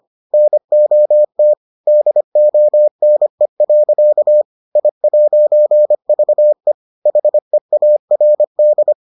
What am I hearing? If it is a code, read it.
Morse code, heard as a keyed tone, about 25 wpm, 600 Hz. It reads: NOT DONE. I'VE HEARD